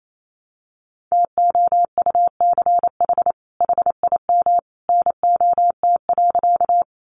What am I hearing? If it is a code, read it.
Morse code, heard as a keyed tone, about 28 wpm, 700 Hz. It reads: TOUCH HIM NOT.
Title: Sentences from Top 1200 Words 28wpm